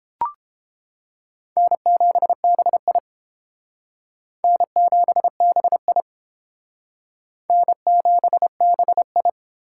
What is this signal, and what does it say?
Morse code, heard as a keyed tone, about 30 wpm, 700 Hz. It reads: N7BI N7BI N7BI